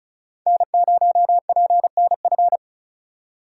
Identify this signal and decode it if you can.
Morse code, heard as a keyed tone, about 34 wpm, 700 Hz. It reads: N0PNF
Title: USA Call Signs - 5 Characters - 2x Character Spacing - 35wpm